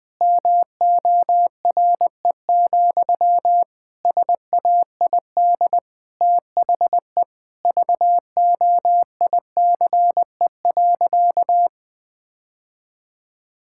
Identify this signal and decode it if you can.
Morse code, heard as a keyed tone, about 20 wpm, 700 Hz. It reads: MORE, SAID THE VOICE.
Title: Sentences from Top 1600 Words 20wpm